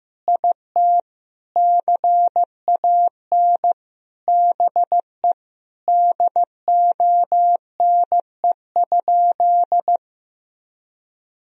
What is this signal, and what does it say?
Morse code, heard as a keyed tone, about 15 wpm, 700 Hz. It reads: IT CAN BE DONE?
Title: Sentences from Top 1000 Words - Review Part 1 - 15wpm